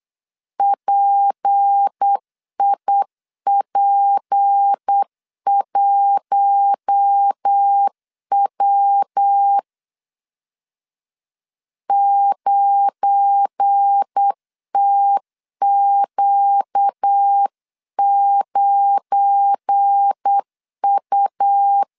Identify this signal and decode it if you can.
Morse code, heard as a keyed tone, about 8 wpm, 800 Hz. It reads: PIP1W 9TQ9U